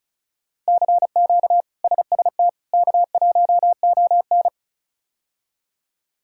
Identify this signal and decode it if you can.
Morse code, heard as a keyed tone, about 34 wpm, 700 Hz. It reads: CQ SST K1ON
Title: K1USN SST QSO Practice - 35wpm